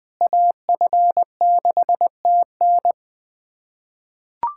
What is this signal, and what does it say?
Morse code, heard as a keyed tone, about 21 wpm, 700 Hz. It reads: AF6TN
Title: USA Callsign DE Callsign - 5 Characters - 20wpm